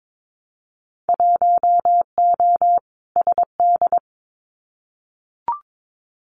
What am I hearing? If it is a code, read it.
Morse code, heard as a keyed tone, about 22 wpm, 700 Hz. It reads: 1O SD
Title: Winter Field Day Exchanges Encore 22wpm